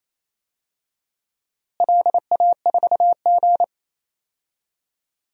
Morse code, heard as a keyed tone, about 28 wpm, 700 Hz. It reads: LA4G
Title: DXCC-100 Call Signs - 4 Characters - Repeated 2x - 28wpm